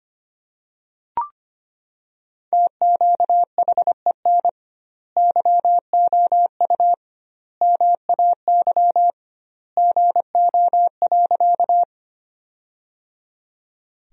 Morse code, heard as a keyed tone, about 25 wpm, 700 Hz. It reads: TQHEN YOU MAY GO.
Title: 2x Word-Spacing - Sentences from Top 100 Words - 25wpm